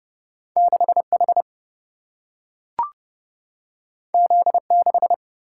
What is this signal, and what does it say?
Morse code, heard as a keyed tone, about 30 wpm, 700 Hz. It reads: 6 H Z 6
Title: Single-Letter Single-Number 30wpm